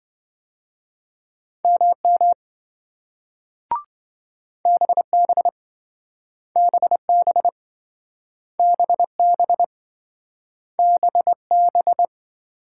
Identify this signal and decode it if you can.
Morse code, heard as a keyed tone, about 27 wpm, 700 Hz. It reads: MM BB BB BB BB